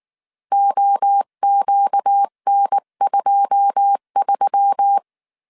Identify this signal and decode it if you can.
Morse code, heard as a keyed tone, about 19 wpm, 800 Hz. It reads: OQN23
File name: CW / DET / 1 9 1500C S C 18wpm 2539